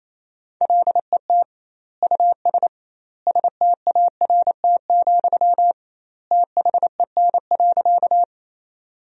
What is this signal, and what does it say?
Morse code, heard as a keyed tone, about 27 wpm, 700 Hz. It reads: LET US START, THEN.
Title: 2x Word-Spacing - Sentences from Top 300 Words - 28wpm